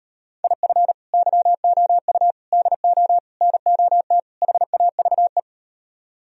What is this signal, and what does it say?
Morse code, heard as a keyed tone, about 38 wpm, 700 Hz. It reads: IF YOU DO NOT HAVE